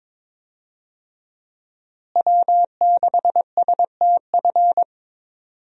Morse code, heard as a keyed tone, about 22 wpm, 700 Hz. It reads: W6STF
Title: USA Call Signs - 5 Characters - Repeated 2x - 22wpm